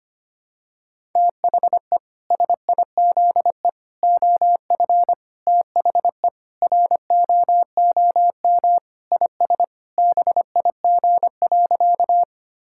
Morse code, heard as a keyed tone, about 25 wpm, 700 Hz. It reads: THE SIZE OF THE ROOM IS BIG.
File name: practice / 25wpm / Sentences with 3-Word Phrases - 25wpm